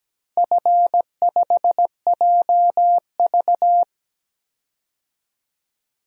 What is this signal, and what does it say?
Morse code, heard as a keyed tone, about 17 wpm, 700 Hz. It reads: F5JV
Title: DXCC-100 Call Signs - 4 Characters - 17wpm